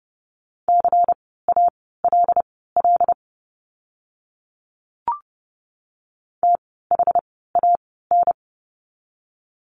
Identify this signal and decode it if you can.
Morse code, heard as a keyed tone, about 30 wpm, spaced as for 17 wpm, 700 Hz. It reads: CALL THAN
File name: course / 30:10wpm / 230 Lesson 37f - Letter x - Words 30wpm